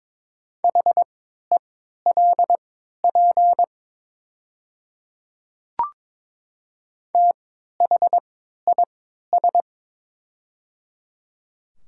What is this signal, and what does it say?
Morse code, heard as a keyed tone, about 22 wpm, spaced as for 12 wpm, 700 Hz. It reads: HELP THIS
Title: 3x Character Spacing - Top 300 Words - 22wpm